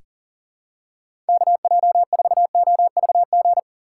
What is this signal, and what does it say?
Morse code, heard as a keyed tone, about 40 wpm, 700 Hz. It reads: KJ4OVG
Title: USA Call Signs - 2x Character Spacing - 40wpm